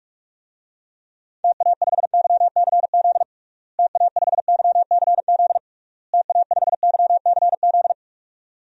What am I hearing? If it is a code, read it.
Morse code, heard as a keyed tone, about 46 wpm, 700 Hz. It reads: TA5YCZ TA5YCZ TA5YCZ
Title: DXCC-100 Call Signs - 6 Characters - Repeated 3x - 45wpm